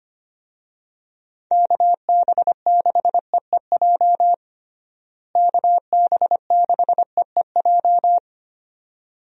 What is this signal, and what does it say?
Morse code, heard as a keyed tone, about 25 wpm, 700 Hz. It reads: KB6EEJ KB6EEJ
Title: USA Call Signs - 6 Characters - Repeated 2x - 25wpm